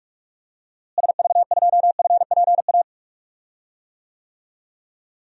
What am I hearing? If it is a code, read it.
Morse code, heard as a keyed tone, about 46 wpm, 700 Hz. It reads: SV2FPA
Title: DXCC-100 Call Signs - 6 Characters - Repeated 3x - 45wpm